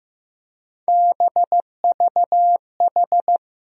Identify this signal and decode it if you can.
Morse code, heard as a keyed tone, about 15 wpm, 700 Hz. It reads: BVH
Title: USA Call Sign Postfixes - 3 Letters 15wpm